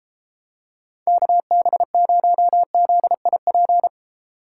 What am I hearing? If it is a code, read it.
Morse code, heard as a keyed tone, about 31 wpm, 700 Hz. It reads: KB0ZIP